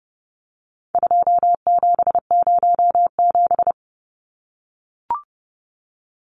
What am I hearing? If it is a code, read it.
Morse code, heard as a keyed tone, about 30 wpm, 700 Hz. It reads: 2707
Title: Numbers - 4-Digit 30wpm